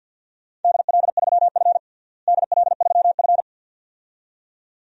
Morse code, heard as a keyed tone, about 47 wpm, 700 Hz. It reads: DL3F DL3F
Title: DXCC-100 Call Signs - 4 Characters - Repeated 2x - 50wpm